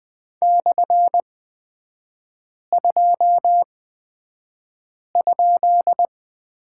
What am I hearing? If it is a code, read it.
Morse code, heard as a keyed tone, about 20 wpm, spaced as for 5 wpm, 700 Hz. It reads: /2?
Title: 224 Lesson 36e - BT - Three Characters 20wpm